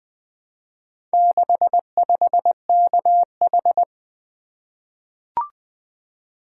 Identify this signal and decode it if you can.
Morse code, heard as a keyed tone, about 20 wpm, 700 Hz. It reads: 65KH
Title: Four Letter-Number Combinations - 20wpm